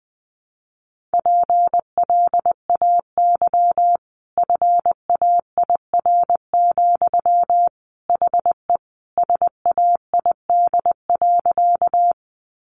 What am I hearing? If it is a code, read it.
Morse code, heard as a keyed tone, about 20 wpm, 700 Hz. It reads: PLAY FAIR, HE SAID.
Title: Sentences from Top 1300 Words 20wpm